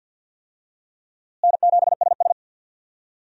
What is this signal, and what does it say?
Morse code, heard as a keyed tone, about 46 wpm, 700 Hz. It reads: N7SS
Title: USA Call Signs - 4 Characters - Repeated 2x - 50wpm